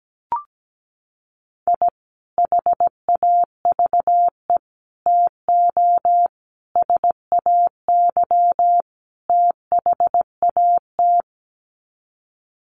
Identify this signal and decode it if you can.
Morse code, heard as a keyed tone, about 17 wpm, 700 Hz. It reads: I HAVE TO SAY THAT
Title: Sets of 5 Words Encore 17wpm